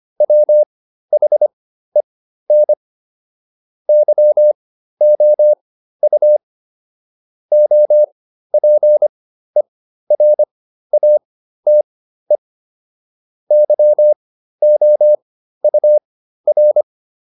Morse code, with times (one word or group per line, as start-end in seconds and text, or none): 0.2-2.7 WHEN
3.9-6.4 YOU
7.5-12.4 OPERATE
13.5-16.8 YOUR